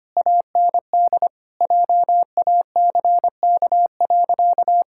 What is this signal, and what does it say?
Morse code, heard as a keyed tone, about 25 wpm, 700 Hz. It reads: AND JACK.
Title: Pangram Challenge v2 - 25wpm